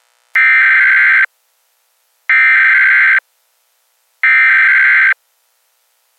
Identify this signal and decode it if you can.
Emergency Alert System (SAME) tones (header bursts), heard as harsh buzzing bursts. It is heard from 0.4 to 5.1 s.